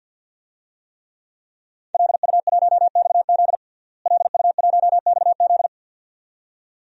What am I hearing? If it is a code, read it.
Morse code, heard as a keyed tone, about 50 wpm, 700 Hz. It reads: LU1XZ LU1XZ
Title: DXCC-100 Call Signs - 5 Characters - Repeated 2x - 50wpm